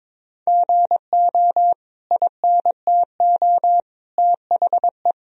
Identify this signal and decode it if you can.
Morse code, heard as a keyed tone, about 22 wpm, 700 Hz. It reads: GO INTO THE
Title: Sentences from Top 1000 Words - Review Part 4 - 22wpm